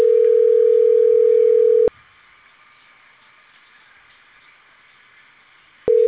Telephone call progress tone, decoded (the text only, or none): ringback tone